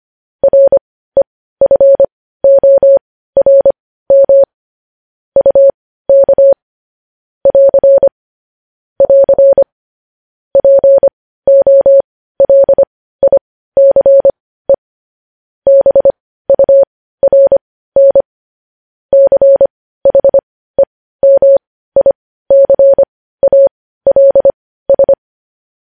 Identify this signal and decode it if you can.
Morse code, heard as a keyed tone, about 25 wpm, spaced as for 15 wpm, 550 Hz. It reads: REFORM UK <AR> <AR> POLICE BURN CHEMICALS
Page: news headlines in morse code at 15 wpm